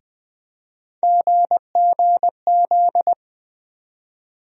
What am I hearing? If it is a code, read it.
Morse code, heard as a keyed tone, about 20 wpm, 700 Hz. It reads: GGZ
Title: USA Call Sign Postfixes - 3 Letters 20wpm